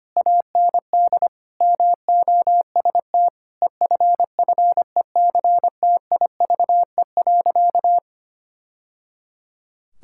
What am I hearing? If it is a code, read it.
Morse code, heard as a keyed tone, about 25 wpm, 700 Hz. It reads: AND MOST EFFECTIVE.
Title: Sentences from Top 1000 Words - Review Part 6 - 25wpm